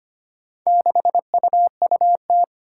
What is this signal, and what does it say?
Morse code, heard as a keyed tone, about 25 wpm, 700 Hz. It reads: 6UUT
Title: Four Challenging Letter-Number Combinations 25wpm